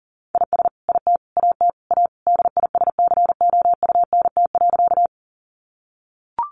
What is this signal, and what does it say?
Morse code, heard as a keyed tone, about 40 wpm, 700 Hz. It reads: IS IT AT A DISCOUNT.
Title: Sentences from Top 1200 Words 40wpm